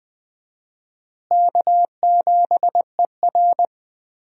Morse code, heard as a keyed tone, about 20 wpm, 700 Hz. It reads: K7ER
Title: USA Call Signs - 4 Characters - Repeated 2x - 20wpm